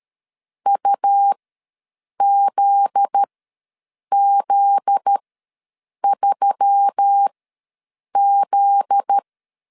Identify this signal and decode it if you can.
Morse code, heard as a keyed tone, about 13 wpm, spaced as for 7 wpm, 800 Hz. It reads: UZZ3Z
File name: 7 GF9Z ws 12wpm